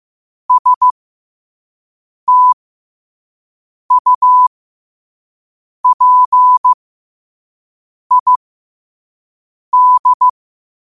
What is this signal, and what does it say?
Morse code, heard as a keyed tone, about 15 wpm, spaced as for 5 wpm, 1000 Hz. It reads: STUPID
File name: qotdfiles / 5wpm